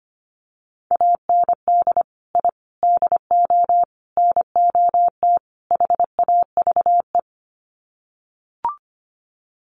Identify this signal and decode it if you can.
Morse code, heard as a keyed tone, about 25 wpm, 700 Hz. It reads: AND I DO NOT HAVE
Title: Sets of 5 Words Encore 25wpm